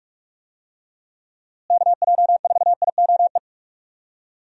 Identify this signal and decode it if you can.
Morse code, heard as a keyed tone, about 46 wpm, 700 Hz. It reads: KJ4IOE
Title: USA Call Signs - 6 Characters - Repeated 2x - 45wpm